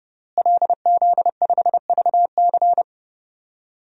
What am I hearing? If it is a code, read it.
Morse code, heard as a keyed tone, about 30 wpm, 700 Hz. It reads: LZ5VC
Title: DXCC-100 Call Signs - 5 Characters - Repeated 3x - 30wpm